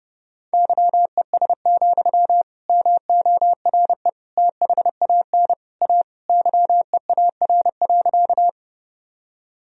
Morse code, heard as a keyed tone, about 32 wpm, 700 Hz. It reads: YES, MORE THAN A YEAR.
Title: Sentences from Top 1000 Words - Review Part 1 - 30wpm